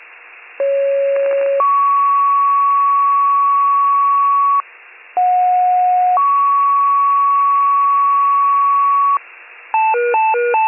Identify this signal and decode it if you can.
Two-tone page, heard as a one-second tone, then a three-second tone, about 550 then 1120 Hz, about 730 then 1120 Hz. It is heard from 0.6 to 9.2 s.